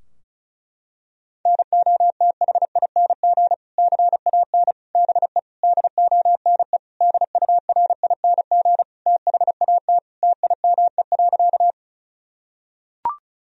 Morse code, heard as a keyed tone, about 34 wpm, 700 Hz. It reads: NOTHING CAN BE DONE DURING THAT TIME.
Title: Sentences from Top 1000 Words - Review Part 2 - 35wpm